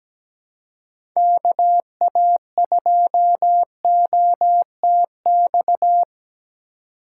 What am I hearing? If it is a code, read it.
Morse code, heard as a keyed tone, about 17 wpm, 700 Hz. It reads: KA2OTX
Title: USA Call Signs - 6 Characters - 2x Character Spacing - 17wpm